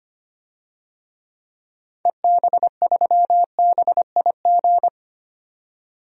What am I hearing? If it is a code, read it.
Morse code, heard as a keyed tone, about 25 wpm, 700 Hz. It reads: EB3BIG
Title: DXCC-100 Call Signs - 6 Characters - 25wpm